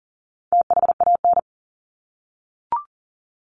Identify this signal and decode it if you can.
Morse code, heard as a keyed tone, about 38 wpm, 700 Hz. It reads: THAN
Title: Top 300 Words Repeated 40wpm